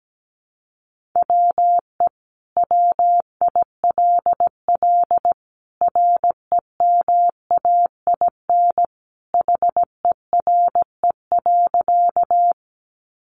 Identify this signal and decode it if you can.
Morse code, heard as a keyed tone, about 17 wpm, 700 Hz. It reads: WE WILL REMAIN HERE.
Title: Sentences from Top 1200 Words 17wpm